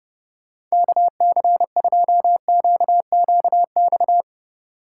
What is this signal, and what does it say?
Morse code, heard as a keyed tone, about 30 wpm, 700 Hz. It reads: KC2QQX
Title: Speed Racing - USA 2x3 Call Signs 25wpm